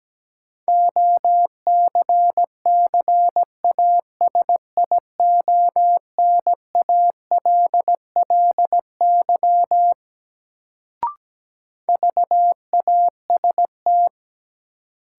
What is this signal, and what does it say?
Morse code, heard as a keyed tone, about 17 wpm, 700 Hz. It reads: OCCASIONALLY VAST
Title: Sentences from Top 1300 Words - New Word Drill - 17wpm